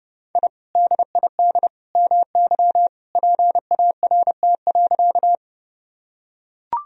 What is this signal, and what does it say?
Morse code, heard as a keyed tone, about 29 wpm, 700 Hz. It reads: I DID MY PART.
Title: Sentences from Top 1000 Words - Review Part 1 - 30wpm